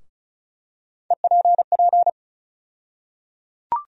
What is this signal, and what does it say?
Morse code, heard as a keyed tone, about 36 wpm, 700 Hz. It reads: EPP
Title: USA Call Sign Postfixes - 3 Letters 35wpm